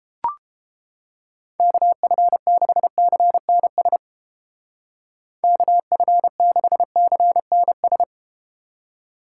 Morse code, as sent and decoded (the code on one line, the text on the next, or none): -.- ..-. -.... -.-. -. ... / -.- ..-. -.... -.-. -. ...
KF6CNS KF6CNS